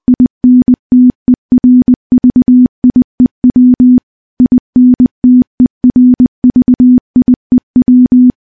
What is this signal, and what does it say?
Morse code, heard as a keyed tone, about 20 wpm, 250 Hz. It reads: INTERVIEW INTERVIEW
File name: morse-training-20WPM